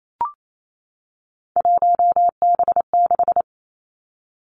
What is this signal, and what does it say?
Morse code, heard as a keyed tone, about 28 wpm, 700 Hz. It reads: 1B6
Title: Three Challenging Letter-Number Combinations 28wpm